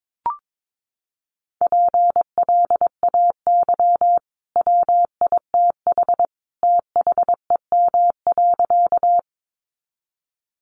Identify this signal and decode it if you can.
Morse code, heard as a keyed tone, about 22 wpm, 700 Hz. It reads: PLAY WITH THEM.
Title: Sentences from Top 200 Words 22wpm